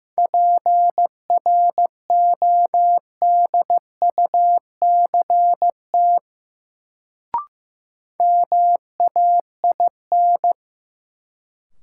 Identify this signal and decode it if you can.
Morse code, heard as a keyed tone, about 15 wpm, 700 Hz. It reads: PRODUCT MAIN
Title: Sentences from Top 1000 Words - Review Part 2 - New Word Drill - 15wpm